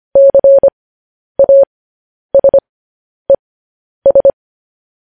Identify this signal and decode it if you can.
Morse code, heard as a keyed tone, about 25 wpm, spaced as for 10 wpm, 550 Hz. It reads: CASES